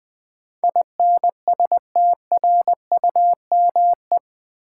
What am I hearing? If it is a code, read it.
Morse code, heard as a keyed tone, about 20 wpm, 700 Hz. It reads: INSTRUME
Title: Build A Word - 4 Letters At A Time - 12-Letter Words - 20wpm